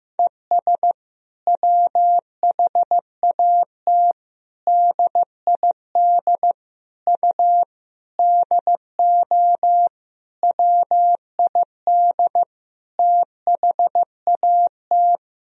Morse code, heard as a keyed tone, about 15 wpm, 700 Hz. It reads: ES WHAT DID U DO WID THAT
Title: Sentences from Top 300 Words and Abbreviations 15wpm